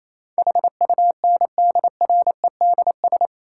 Morse code, heard as a keyed tone, about 28 wpm, 700 Hz. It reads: HUNDREDS
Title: Sets of 4 Words - New Word Drill - 28wpm